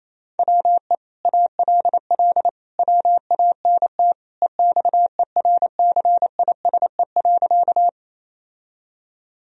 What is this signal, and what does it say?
Morse code, heard as a keyed tone, about 28 wpm, 700 Hz. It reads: WE ALL WANT EXERCISE.
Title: Sentences from Top 1000 Words - Review Part 4 - 28wpm